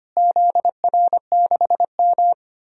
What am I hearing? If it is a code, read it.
Morse code, heard as a keyed tone, about 25 wpm, 700 Hz. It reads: ZR6M
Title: DXCC-100 Call Signs - 4 Characters - Repeated 2x - 25wpm